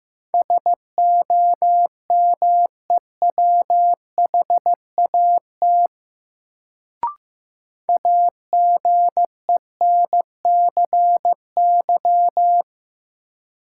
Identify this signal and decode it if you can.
Morse code, heard as a keyed tone, about 15 wpm, 700 Hz. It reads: SOMEWHAT AGENCY
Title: Sentences from Top 1300 Words - New Word Drill Alt - 15wpm